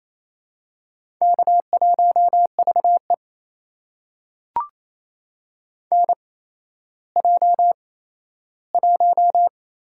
Morse code, heard as a keyed tone, about 28 wpm, spaced as for 14 wpm, 700 Hz. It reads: K1VE NJ1
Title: USA Call Signs - 4 Characters - 8x Character Spacing - 28wpm